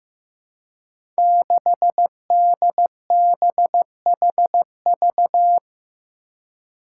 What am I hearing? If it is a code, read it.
Morse code, heard as a keyed tone, about 16 wpm, 700 Hz. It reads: 6DBHV
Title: Five Challenging Letter-Number Combinations 15wpm